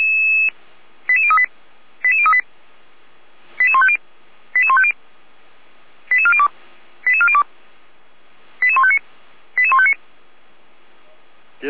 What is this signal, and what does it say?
Five-tone selective call (ZVEI), heard as quick runs of five tones, tones of about 70 ms, with speech. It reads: 88038 88038 88150 88150 88402 88402 88159 88159